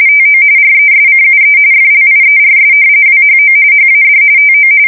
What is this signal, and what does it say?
RTTY, about 45.45 baud. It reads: TO RESCUE FLIERS WHO HAD TO
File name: fsk170 / battleshipnj